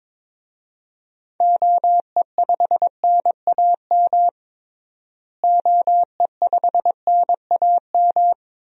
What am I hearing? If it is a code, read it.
Morse code, heard as a keyed tone, about 21 wpm, 700 Hz. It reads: OE5NAM OE5NAM
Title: DXCC-100 Call Signs - 6 Characters - Repeated 2x - 22wpm